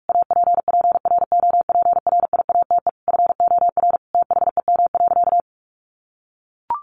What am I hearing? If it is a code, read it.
Morse code, heard as a keyed tone, about 44 wpm, 700 Hz. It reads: APPROPRIATE FOR THEM.
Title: Sentences from Top 900 Words 45wpm